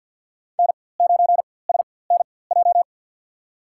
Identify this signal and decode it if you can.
Morse code, heard as a keyed tone, about 47 wpm, spaced as for 23 wpm, 700 Hz. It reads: N9SNJ
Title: USA Call Signs - 5 Characters - 4x Character Spacing - 50wpm